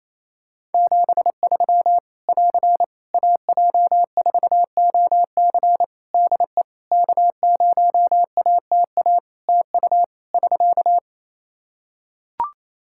Morse code, heard as a keyed tone, about 28 wpm, 700 Hz. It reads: 73 <AR> AJ4OC DE K0ATA TU <SK>
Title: Practice QSO - Exchange 4 - Ending - 28wpm